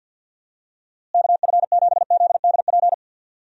Morse code, heard as a keyed tone, about 52 wpm, 700 Hz. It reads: KF7ZDP
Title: USA Call Signs - 6 Characters - Repeated 3x - 50wpm